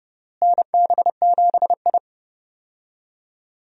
Morse code, heard as a keyed tone, about 32 wpm, 700 Hz. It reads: NB7I
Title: USA Call Signs - 4 Characters - Repeated 3x - 30wpm